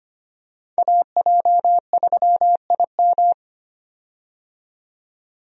Morse code, heard as a keyed tone, about 25 wpm, 700 Hz. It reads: AJ3IM
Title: USA Call Signs - 5 Characters - 25wpm